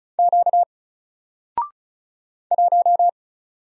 Morse code, heard as a keyed tone, about 36 wpm, 700 Hz. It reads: Q 1
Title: Single Character (Letter:Number:Punctuation:Prosign) Alt 35wpm